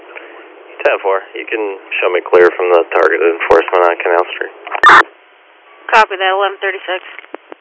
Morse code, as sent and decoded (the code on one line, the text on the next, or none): none